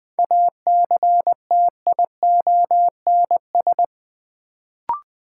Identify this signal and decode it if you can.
Morse code, heard as a keyed tone, about 21 wpm, 700 Hz. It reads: ACTIONS
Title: Sentences from Top 1000 Words - Review Part 6 - New Word Drill - 20wpm